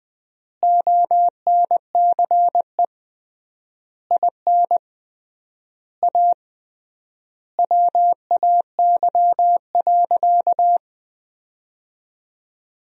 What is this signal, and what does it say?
Morse code, heard as a keyed tone, about 20 wpm, 700 Hz. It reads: ONCE IN A WAY.
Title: 3x Word-Spacing - Sentences from Top 500 Words - 20wpm